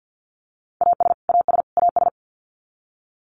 Morse code, heard as a keyed tone, about 48 wpm, 700 Hz. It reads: AS AS AS
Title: Top 200 Words Repeated 50wpm